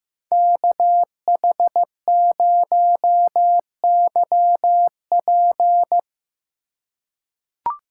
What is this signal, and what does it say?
Morse code, heard as a keyed tone, about 15 wpm, 700 Hz. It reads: KH0YP
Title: USA Call Signs - 5 Characters - 2x Character Spacing - 15wpm